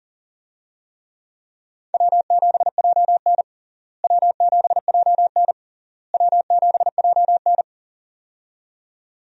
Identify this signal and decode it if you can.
Morse code, heard as a keyed tone, about 40 wpm, 700 Hz. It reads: W7JN W7JN W7JN
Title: USA Call Signs - 4 Characters - Repeated 3x - 40wpm